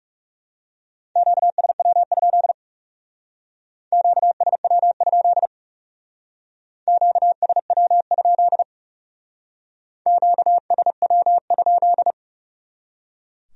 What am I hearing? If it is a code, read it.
Morse code, heard as a keyed tone, about 38 wpm, 700 Hz. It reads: QSW? QSW? QSW? QSW?